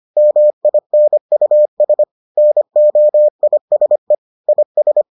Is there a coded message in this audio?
Yes: Morse code from 0.2 to 5.0 s, heard as a keyed tone, about 25 wpm, 600 Hz. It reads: MINUS NOISE IS